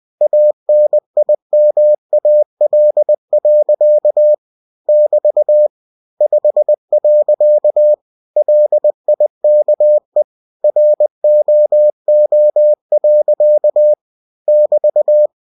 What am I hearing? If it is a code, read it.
Morse code, heard as a keyed tone, about 20 wpm, 600 Hz. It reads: ANIMAL. <BT> 5. LIKE ROO. <BT>
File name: cw-books / winnie-the-pooh / winnie-20wpm-page-0064